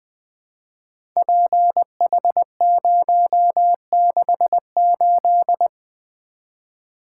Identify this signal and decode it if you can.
Morse code, heard as a keyed tone, about 20 wpm, 700 Hz. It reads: PH068
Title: Code Groups - 5 Letter:Numbers - 20wpm